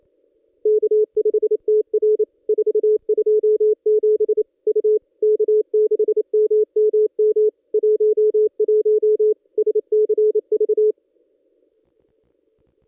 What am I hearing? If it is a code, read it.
Morse code, heard as a keyed tone, about 28 wpm, 430 Hz. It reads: K5TR 427 U K6MMM 11 SCV
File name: contests / ss-cw-k5tr-op-n5rz -1600